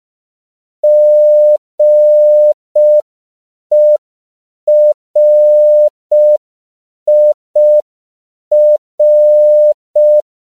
Morse code, heard as a keyed tone, about 5 wpm, 600 Hz. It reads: GERIR
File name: CW / 5PPM